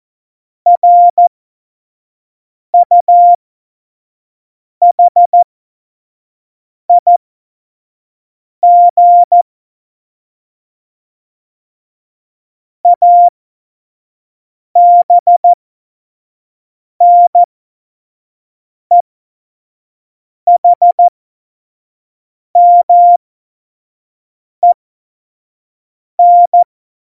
Morse code, heard as a keyed tone, about 14 wpm, spaced as for 5 wpm, 700 Hz. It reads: RUHIG ABNEHMEN
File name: Morsekurs / Witze / Witz 23. Januar 2018, 25 F70